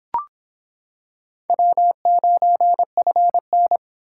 Morse code, heard as a keyed tone, about 26 wpm, 700 Hz. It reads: W9FN